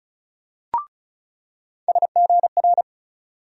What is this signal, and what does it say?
Morse code, heard as a keyed tone, about 34 wpm, 700 Hz. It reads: SGR
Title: USA Call Sign Postfixes - 3 Letters 35wpm